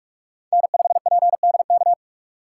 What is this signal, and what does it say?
Morse code, heard as a keyed tone, about 46 wpm, 700 Hz. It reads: N5PDK
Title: USA Call Signs - 5 Characters - 2x Character Spacing - 45wpm